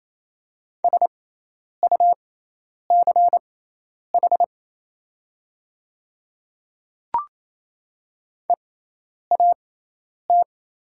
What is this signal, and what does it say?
Morse code, heard as a keyed tone, about 27 wpm, spaced as for 10 wpm, 700 Hz. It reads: SUCH EAT
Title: 6x Character Spacing - Top 300 Words - 28wpm